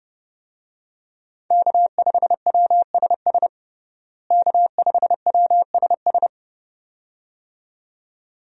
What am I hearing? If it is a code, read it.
Morse code, heard as a keyed tone, about 30 wpm, 700 Hz. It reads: K5WSS K5WSS
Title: USA Call Signs - 5 Characters - Repeated 2x - 30wpm